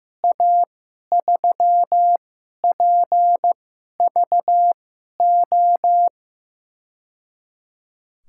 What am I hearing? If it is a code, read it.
Morse code, heard as a keyed tone, about 15 wpm, spaced as for 11 wpm, 700 Hz. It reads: A3PVO
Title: USA Call Signs - 6 Characters - 2x Character Spacing - 15wpm